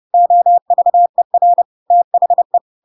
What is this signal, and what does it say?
Morse code, heard as a keyed tone, about 30 wpm, 700 Hz. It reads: OVER THE
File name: Pinocchio / Pinocchio ch3 30 30 ik1hge